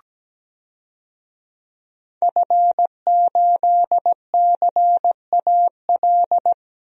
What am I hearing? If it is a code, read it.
Morse code, heard as a keyed tone, about 17 wpm, 700 Hz. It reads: F8CAL
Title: DXCC-100 Call Signs - 5 Characters - 17wpm